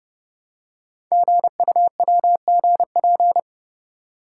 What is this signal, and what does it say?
Morse code, heard as a keyed tone, about 29 wpm, 690 Hz. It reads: GUWGP
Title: Code Groups - 5 Letters - 30wpm